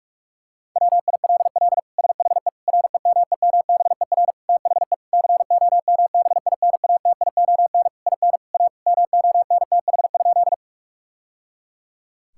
Morse code, heard as a keyed tone, about 46 wpm, 700 Hz. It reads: WILL SHE REMEMBER THE COMBINATION IN A MONTH?
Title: Sentences from Top 1000 Words - Review Part 6 - 45wpm